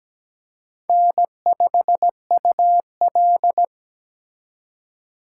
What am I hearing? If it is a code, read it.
Morse code, heard as a keyed tone, about 20 wpm, 700 Hz. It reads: N5UL
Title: Speed Racing - USA 1x2 Call Signs 17wpm